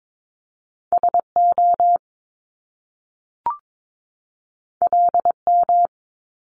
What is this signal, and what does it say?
Morse code, heard as a keyed tone, about 22 wpm, 700 Hz. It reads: SO LM
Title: Top Two Letter Combinations in Words 22wpm